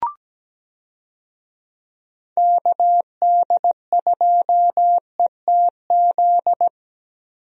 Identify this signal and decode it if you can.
Morse code, heard as a keyed tone, about 17 wpm, 700 Hz. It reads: KD2ETZ
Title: USA Call Signs - 6 Characters - Repeated 3x - 17wpm